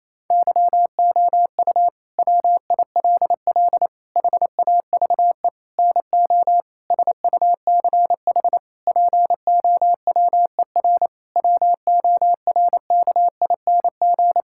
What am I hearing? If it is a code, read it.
Morse code, heard as a keyed tone, about 27 wpm, 700 Hz. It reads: YOU WILL HAVE NO SUCH POWER WORKING
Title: Sentences from Top 1000 Words - Review Part 6 - 28wpm